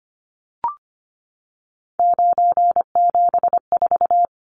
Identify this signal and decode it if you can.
Morse code, heard as a keyed tone, about 25 wpm, 700 Hz. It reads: 974
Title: Numbers - 3-Digit 25wpm